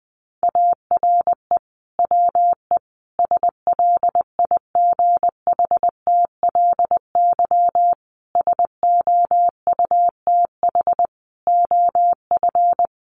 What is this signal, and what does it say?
Morse code, heard as a keyed tone, about 19 wpm, 700 Hz. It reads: ARE WE SLIGHTLY SOUTH OF
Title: Sentences from Top 1300 Words 20wpm